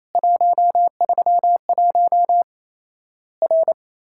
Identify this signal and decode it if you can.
Morse code, heard as a keyed tone, about 28 wpm, 700 Hz. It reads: 131 R
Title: Worked All Europe DX Contest - QTC Exchange - 28wpm